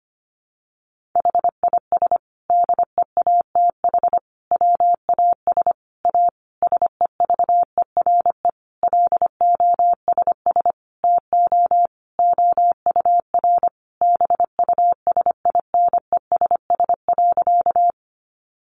Morse code, heard as a keyed tone, about 25 wpm, 700 Hz. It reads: HIS DEATH WAS A SEVERE LOSS TO OUR BUSINESS.
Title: Sentences from Top 1400 Words 25wpm